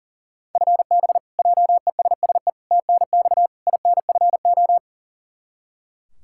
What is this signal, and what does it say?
Morse code, heard as a keyed tone, about 40 wpm, 700 Hz. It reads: FB JESSE TNX INFO
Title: Practice QSO - Exchange 3 - More Info - 40wpm